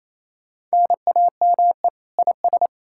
Morse code, heard as a keyed tone, about 38 wpm, 700 Hz. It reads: NAME IS